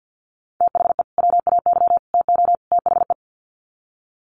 Morse code, heard as a keyed tone, about 48 wpm, 700 Hz. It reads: THE WAY TO THE
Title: Sets of 4 Words 50wpm